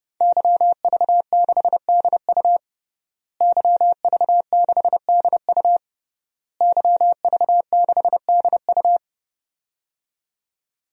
Morse code, heard as a keyed tone, about 31 wpm, 700 Hz. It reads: YV6DU YV6DU YV6DU
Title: DXCC-100 Call Signs - 5 Characters - Repeated 3x - 30wpm